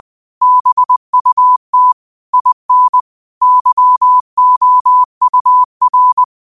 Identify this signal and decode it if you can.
Morse code, heard as a keyed tone, about 20 wpm, 1000 Hz. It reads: BUT IN YOUR
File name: qotdfiles / 20wpm